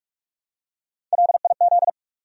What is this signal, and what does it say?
Morse code, heard as a keyed tone, about 46 wpm, 700 Hz. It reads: LIZ
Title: USA Common Names - Max 3 Letters - 45wpm